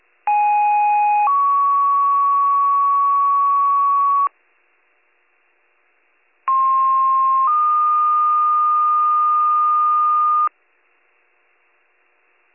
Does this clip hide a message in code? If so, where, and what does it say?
Yes: two-tone page from 0.3 to 10.5 s, heard as a one-second tone, then a three-second tone, about 830 then 1120 Hz, about 1060 then 1220 Hz.